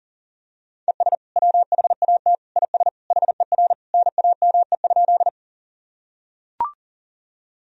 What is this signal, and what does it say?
Morse code, heard as a keyed tone, about 40 wpm, 700 Hz. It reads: ES WHAT IS HER NAME?
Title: Sentences from Top 200 Words and Abbreviations 40wpm